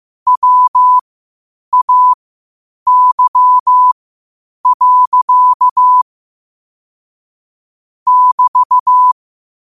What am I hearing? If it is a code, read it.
Morse code, heard as a keyed tone, about 15 wpm, spaced as for 8 wpm, 1000 Hz. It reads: WAY. <BT>
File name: 8wpm